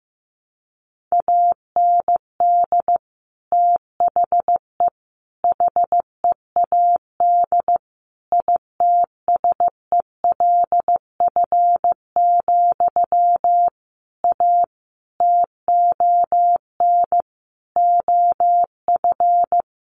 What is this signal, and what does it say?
Morse code, heard as a keyed tone, about 15 wpm, 700 Hz. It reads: AND THE HEAD ITSELF, A TON OF
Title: A Sound Of Thunder 15wpm